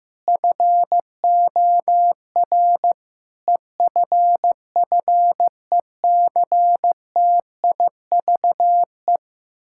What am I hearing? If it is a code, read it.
Morse code, heard as a keyed tone, about 15 wpm, 700 Hz. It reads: FOR EFFECTIVE
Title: Sentences from Top 1000 Words - Review Part 6 - 15wpm